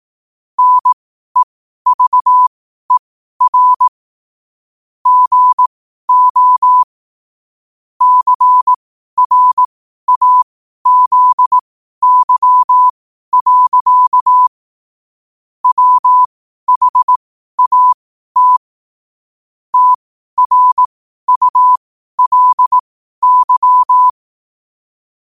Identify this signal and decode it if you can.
Morse code, heard as a keyed tone, about 18 wpm, spaced as for 12 wpm, 1000 Hz. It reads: NEVER GO CRAZY. WHAT TRULY
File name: qotdfiles / 12wpm